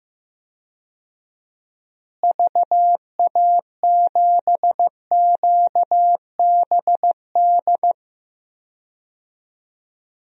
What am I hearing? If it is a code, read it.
Morse code, heard as a keyed tone, about 15 wpm, 700 Hz. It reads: VA7QBD
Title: DXCC-100 Call Signs - 6 Characters - Repeated 3x - 15wpm